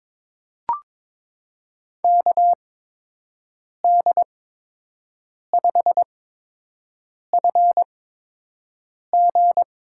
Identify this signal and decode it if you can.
Morse code, heard as a keyed tone, about 22 wpm, spaced as for 6 wpm, 700 Hz. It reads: KD5FG
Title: USA Call Signs - 8x Character Spacing - 22wpm